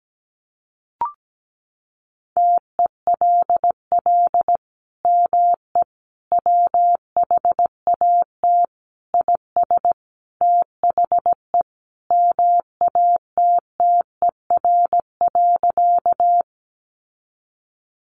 Morse code, heard as a keyed tone, about 17 wpm, 700 Hz. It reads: TELL ME WHAT IS THE MATTER.
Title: Sentences from Top 700 Words 17wpm